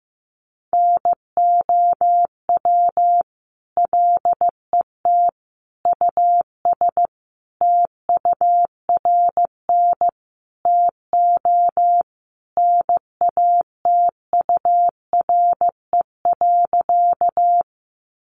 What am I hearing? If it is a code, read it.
Morse code, heard as a keyed tone, about 15 wpm, 700 Hz. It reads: NOW LET US TURN TO NATURE.
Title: Sentences from Top 600 Words 15wpm